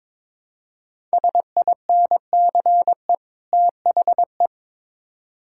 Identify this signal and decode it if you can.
Morse code, heard as a keyed tone, about 22 wpm, 700 Hz. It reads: SINCE THE
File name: Build A Phrase - 2-Word Phrases - 22wpm